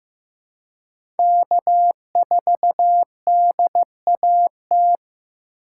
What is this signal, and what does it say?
Morse code, heard as a keyed tone, about 15 wpm, 700 Hz. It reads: K4DAT
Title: USA Call Signs - 5 Characters - 2x Character Spacing - 15wpm